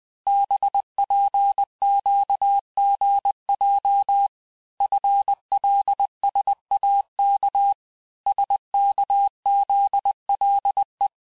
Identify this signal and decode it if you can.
Morse code, heard as a keyed tone, about 20 wpm, 800 Hz. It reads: BPQGJ FLSAK SKZLE
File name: cours-du-lundi files / cours-1584-teste-chiffre-vitesse-19mpm